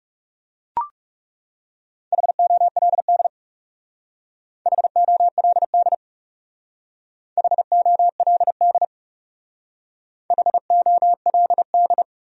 Morse code, heard as a keyed tone, about 38 wpm, 700 Hz. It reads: HOLD HOLD HOLD HOLD